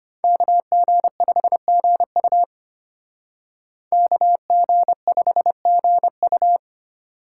Morse code, heard as a keyed tone, about 33 wpm, 700 Hz. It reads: KG5GU KG5GU